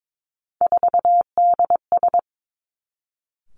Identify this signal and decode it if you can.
Morse code, heard as a keyed tone, about 22 wpm, 700 Hz. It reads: 4DS
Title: Three Challenging Letter-Number Combinations 22wpm